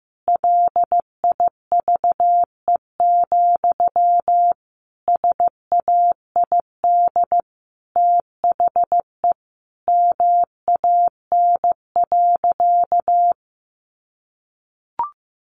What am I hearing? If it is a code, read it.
Morse code, heard as a keyed tone, about 15 wpm, 700 Hz. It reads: LIVE, SAID THE MAN.
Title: Sentences from Top 200 Words 15wpm